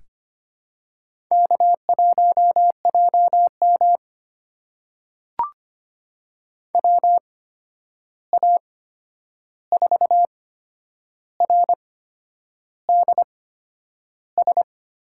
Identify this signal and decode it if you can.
Morse code, heard as a keyed tone, about 25 wpm, spaced as for 9 wpm, 700 Hz. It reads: K1JM WA4RDS
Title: USA Call Signs - 8x Character Spacing - 25wpm